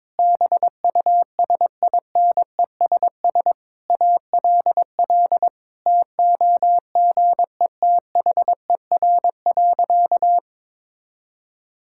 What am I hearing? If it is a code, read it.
Morse code, heard as a keyed tone, about 22 wpm, 700 Hz. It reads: BUSINESS ALL TOGETHER.
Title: Sentences from Top 1000 Words - Review Part 6 - 22wpm